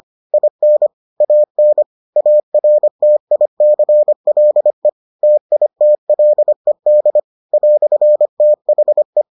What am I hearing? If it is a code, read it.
Morse code, heard as a keyed tone, about 25 wpm, 600 Hz. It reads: IN AN ARTICLE TITLED "THE